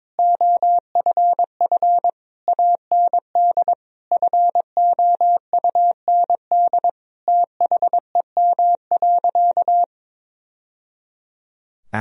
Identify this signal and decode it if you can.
Morse code, heard as a keyed tone, about 22 wpm, 700 Hz. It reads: OFF AND FOUND THEM.
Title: Sentences from Top 1000 Words - Review Part 1 - 22wpm